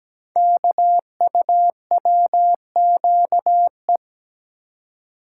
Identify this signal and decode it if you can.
Morse code, heard as a keyed tone, about 17 wpm, 700 Hz. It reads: KUWQE
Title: Code Groups - 5 Letters - 17wpm